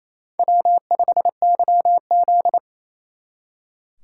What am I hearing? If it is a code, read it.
Morse code, heard as a keyed tone, about 32 wpm, 700 Hz. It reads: W5YZ
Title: Speed Racing - USA 1x2 Call Signs 28wpm